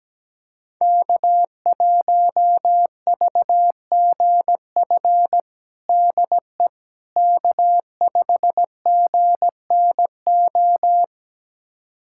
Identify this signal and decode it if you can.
Morse code, heard as a keyed tone, about 17 wpm, 700 Hz. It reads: K1VGF DE K5GNO
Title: USA Callsign DE Callsign - 5 Characters - 17wpm